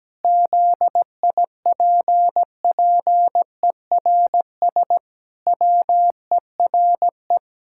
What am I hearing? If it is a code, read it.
Morse code, heard as a keyed tone, about 17 wpm, 700 Hz. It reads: ZIPPERS WERE
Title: Pangram Challenge v2 - 17wpm